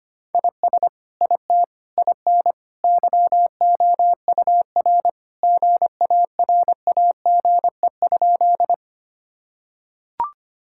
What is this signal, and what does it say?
Morse code, heard as a keyed tone, about 25 wpm, 700 Hz. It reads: IS IT IN YOUR GARAGE?
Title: Sentences from Top 1600 Words 25wpm